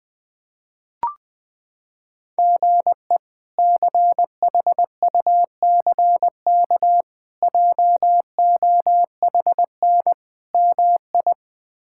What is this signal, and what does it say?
Morse code, heard as a keyed tone, about 21 wpm, 700 Hz. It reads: GE CHUCK JOHN MI
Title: K1USN SST QSO Practice - 20wpm